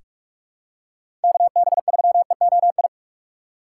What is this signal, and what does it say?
Morse code, heard as a keyed tone, about 41 wpm, 700 Hz. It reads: KB3EOI
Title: USA Call Signs - 6 Characters - 3x Character Spacing - 45wpm